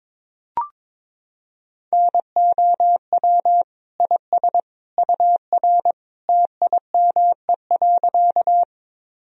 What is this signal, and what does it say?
Morse code, heard as a keyed tone, about 22 wpm, 700 Hz. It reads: NOW IS UR TIME.
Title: Sentences from Top 100 Words and Abbreviations 22wpm